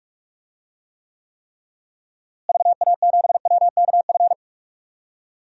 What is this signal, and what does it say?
Morse code, heard as a keyed tone, about 44 wpm, 700 Hz. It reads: VA7WKF
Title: DXCC-100 Call Signs - 6 Characters - Repeated 3x - 45wpm